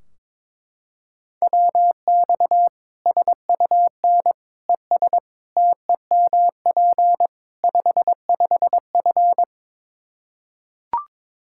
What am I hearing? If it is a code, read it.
Morse code, heard as a keyed tone, about 22 wpm, 700 Hz. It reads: WX SUN ES TEMP 55F
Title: Practice QSO - Exchange 2 - Info - 22wpm